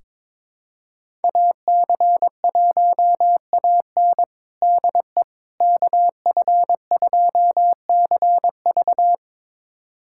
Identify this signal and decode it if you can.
Morse code, heard as a keyed tone, about 22 wpm, 700 Hz. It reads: AC1AN DE KF2CV
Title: USA Callsign DE Callsign - 5 Characters - 22wpm